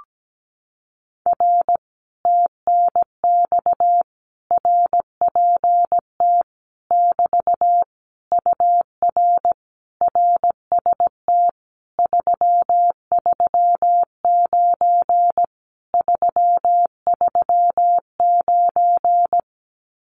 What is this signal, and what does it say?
Morse code, heard as a keyed tone, about 17 wpm, 700 Hz. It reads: R TNX RPT <BT> UR RST 339 339
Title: Common QSO Phrases Encore 17wpm